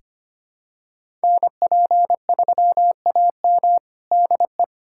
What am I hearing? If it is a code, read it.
Morse code, heard as a keyed tone, about 25 wpm, 700 Hz. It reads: NP3AM DE
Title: USA Callsign DE Callsign 25wpm